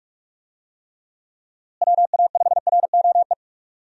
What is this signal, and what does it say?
Morse code, heard as a keyed tone, about 47 wpm, 700 Hz. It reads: WA5ROE
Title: USA Call Signs - 6 Characters - Repeated 3x - 45wpm